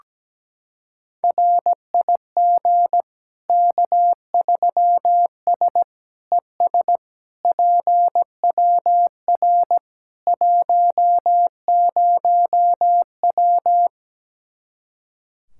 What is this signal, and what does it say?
Morse code, heard as a keyed tone, about 17 wpm, 700 Hz. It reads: RIG K3S ES PWR 10W
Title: Practice QSO - Exchange 2 - Info - 17wpm